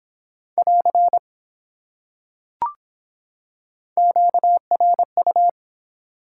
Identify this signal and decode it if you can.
Morse code, heard as a keyed tone, about 24 wpm, 700 Hz. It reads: <AR> QRU